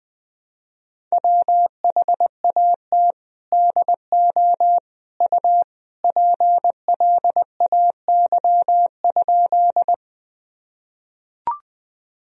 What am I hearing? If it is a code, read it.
Morse code, heard as a keyed tone, about 20 wpm, 700 Hz. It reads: WHAT DO U PLAY?
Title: Sentences from Top 200 Words and Abbreviations 20wpm